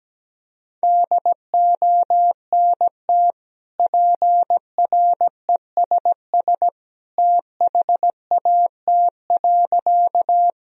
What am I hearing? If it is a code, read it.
Morse code, heard as a keyed tone, about 17 wpm, 700 Hz. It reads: DONT PRESS THAT.
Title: Sentences from Top 1000 Words - Review Part 1 - 17wpm